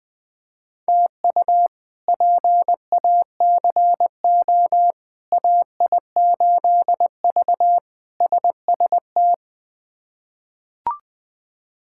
Morse code, heard as a keyed tone, about 20 wpm, 700 Hz. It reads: TU PACO AI8V SST
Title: K1USN SST QSO Practice - 20wpm